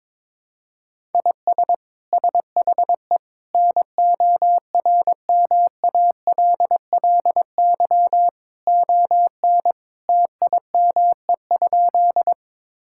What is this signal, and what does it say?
Morse code, heard as a keyed tone, about 22 wpm, 700 Hz. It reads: IS SHE NORMALLY ON TIME?
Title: Sentences from Top 1000 Words - Review Part 6 - 22wpm